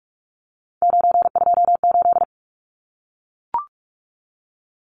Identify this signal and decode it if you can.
Morse code, heard as a keyed tone, about 44 wpm, 700 Hz. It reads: 928
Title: Numbers - 3-Digit 45wpm